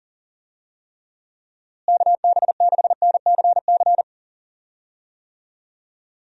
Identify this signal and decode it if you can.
Morse code, heard as a keyed tone, about 37 wpm, 700 Hz. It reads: KB6NCC
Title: USA Call Signs - 6 Characters - Repeated 2x - 40wpm